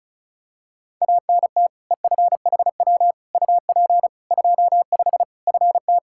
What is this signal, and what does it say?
Morse code, heard as a keyed tone, about 34 wpm, 700 Hz. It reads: ANT EFHW UP 25 FT
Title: Practice QSO - Exchange 2 - Info - 35wpm